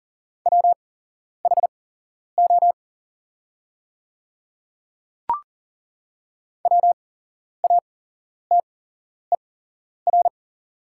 Morse code, heard as a keyed tone, about 40 wpm, spaced as for 11 wpm, 690 Hz. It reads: WHO WATER
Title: 8x Character Spacing - Top 100 Words - 40wpm